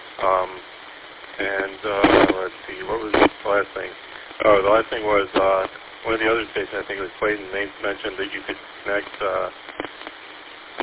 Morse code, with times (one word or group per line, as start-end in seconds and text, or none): none